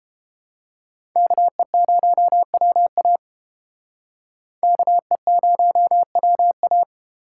Morse code, heard as a keyed tone, about 30 wpm, 700 Hz. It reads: KE0WA KE0WA